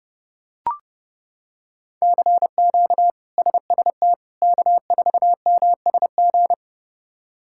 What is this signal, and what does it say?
Morse code, heard as a keyed tone, about 30 wpm, 700 Hz. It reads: CQ SST K4MSG
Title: K1USN SST QSO Practice - 30wpm